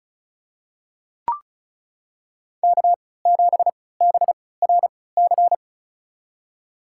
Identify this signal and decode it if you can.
Morse code, heard as a keyed tone, about 36 wpm, spaced as for 20 wpm, 700 Hz. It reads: K7BRC
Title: USA Call Signs - 3x Character Spacing - 35wpm